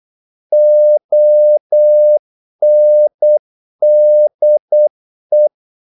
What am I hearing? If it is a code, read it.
Morse code, heard as a keyed tone, about 8 wpm, 600 Hz. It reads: ONDE
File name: Korte ord DK 8wpm